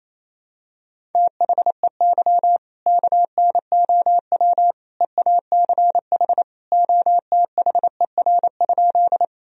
Morse code, heard as a keyed tone, about 28 wpm, 700 Hz. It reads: THEY KNOW EACH OTHER?
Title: Sentences from Top 1000 Words - Review Part 1 - 28wpm